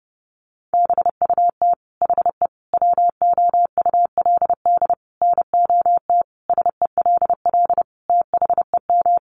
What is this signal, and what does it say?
Morse code, heard as a keyed tone, about 28 wpm, 700 Hz. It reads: BUT HE WOULD NOT SELL THEM
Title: Sentences from Top 1100 Words 30wpm